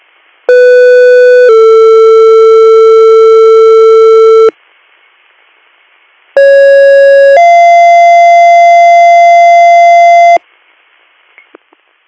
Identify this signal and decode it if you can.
Two-tone page, heard as a one-second tone, then a three-second tone, about 500 then 450 Hz, about 550 then 690 Hz. It is heard from 0.5 to 10.4 s.